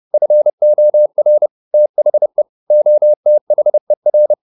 Morse code, heard as a keyed tone, about 30 wpm, 600 Hz. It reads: FOR THE OTHER